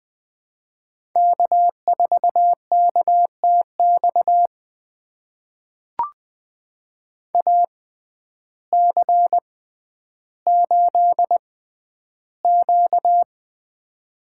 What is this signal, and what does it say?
Morse code, heard as a keyed tone, about 20 wpm, spaced as for 11 wpm, 700 Hz. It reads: K4KTX AC8Q
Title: USA Call Signs - 6x Character Spacing - 20wpm